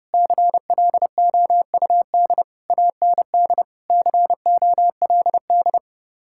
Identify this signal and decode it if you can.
Morse code, heard as a keyed tone, about 37 wpm, 700 Hz. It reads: CLOUD AND COLD
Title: Speed-Racing Common QSO Phrases - 30wpm